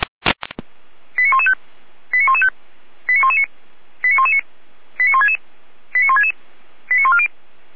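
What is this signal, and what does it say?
Five-tone selective call (ZVEI), heard as quick runs of five tones, tones of about 70 ms. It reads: 89115 89115 89119 89119 89166 89166 89140
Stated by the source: ZVEI five-tone call received on SDR.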